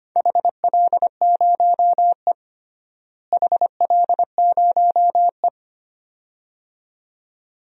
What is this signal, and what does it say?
Morse code, heard as a keyed tone, about 25 wpm, 700 Hz. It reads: HL0E HL0E
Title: DXCC-100 Call Signs - 4 Characters - Repeated 2x - 25wpm